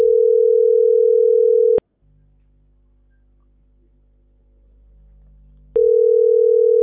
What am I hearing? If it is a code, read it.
Ringback tone (North America), 440 + 480 Hz.